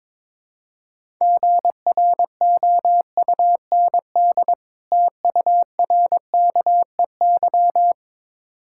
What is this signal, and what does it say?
Morse code, heard as a keyed tone, about 22 wpm, 700 Hz. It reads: GROUND TURKEY
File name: Difficult Words with Hints - 22wpm